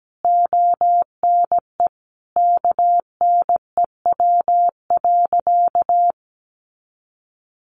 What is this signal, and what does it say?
Morse code, heard as a keyed tone, about 17 wpm, 700 Hz. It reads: ONE KNEW.
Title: Sentences from Top 400 Words 17wpm